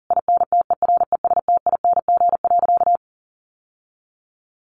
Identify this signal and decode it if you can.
Morse code, heard as a keyed tone, about 40 wpm, 700 Hz. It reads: INTERESTING.
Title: Sentences from Top 1200 Words 40wpm